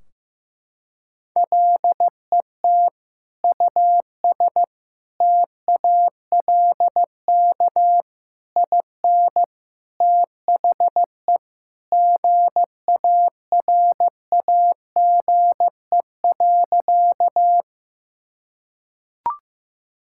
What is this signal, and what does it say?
Morse code, heard as a keyed tone, about 15 wpm, 700 Hz. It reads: LET US TALK IN THE GARAGE.
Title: Sentences from Top 1600 Words 15wpm